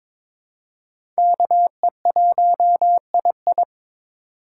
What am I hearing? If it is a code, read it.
Morse code, heard as a keyed tone, about 23 wpm, 700 Hz. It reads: KE1II
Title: USA Call Signs - 5 Characters - 2x Character Spacing - 22wpm